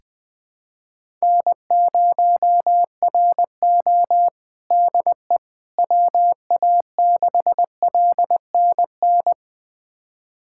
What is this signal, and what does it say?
Morse code, heard as a keyed tone, about 21 wpm, 700 Hz. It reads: N0RO DE WA6LNN
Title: USA Callsign DE Callsign 20wpm